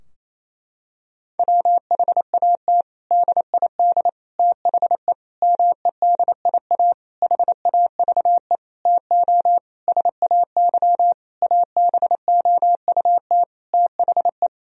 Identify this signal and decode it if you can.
Morse code, heard as a keyed tone, about 28 wpm, 700 Hz. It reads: WHAT DID THE MEDIA HAVE TO SAY ABOUT THE
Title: Sentences from Top 1000 Words - Review Part 4 - 28wpm